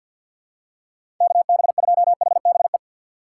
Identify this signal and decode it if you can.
Morse code, heard as a keyed tone, about 53 wpm, 700 Hz. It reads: KB2HBE
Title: USA Call Signs - 6 Characters - 3x Character Spacing - 50wpm